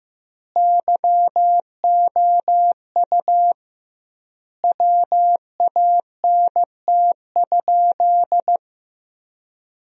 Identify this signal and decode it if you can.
Morse code, heard as a keyed tone, about 15 wpm, 700 Hz. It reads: YOU WANT?
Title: 2x Word-Spacing - Sentences from Top 200 Words - 15wpm